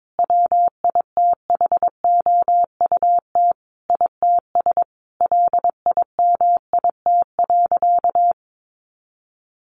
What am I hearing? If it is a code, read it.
Morse code, heard as a keyed tone, about 22 wpm, 700 Hz. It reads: WITHOUT ITS LIMIT.
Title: Sentences from Top 1500 Words 22wpm